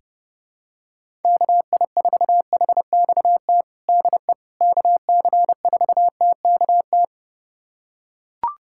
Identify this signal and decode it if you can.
Morse code, heard as a keyed tone, about 32 wpm, 700 Hz. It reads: KI4HXT DE KC4TKT
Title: USA Callsign DE Callsign - 6 Characters - 30wpm